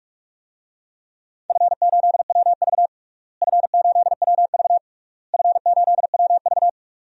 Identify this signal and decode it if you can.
Morse code, heard as a keyed tone, about 42 wpm, 700 Hz. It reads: F8WV F8WV F8WV
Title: DXCC-100 Call Signs - 4 Characters - Repeated 3x - 45wpm